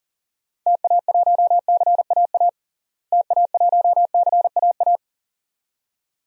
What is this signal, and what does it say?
Morse code, heard as a keyed tone, about 40 wpm, 700 Hz. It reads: TA1CAA TA1CAA
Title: DXCC-100 Call Signs - 6 Characters - Repeated 2x - 40wpm